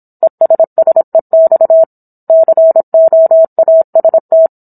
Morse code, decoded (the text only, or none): ESSEX COAST